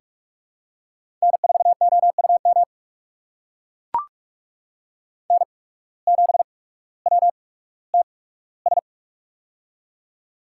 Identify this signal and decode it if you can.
Morse code, heard as a keyed tone, about 44 wpm, spaced as for 19 wpm, 700 Hz. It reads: N4OUM N7WTS
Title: USA Call Signs - 5 Characters - 8x Character Spacing - 45wpm